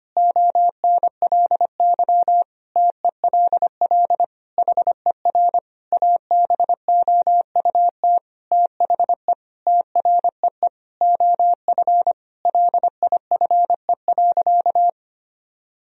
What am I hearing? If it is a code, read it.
Morse code, heard as a keyed tone, about 25 wpm, 700 Hz. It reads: ONLY TELL HER ABOUT THE TREE OF LIFE.
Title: Sentences from Top 1000 Words - Review Part 1 - 25wpm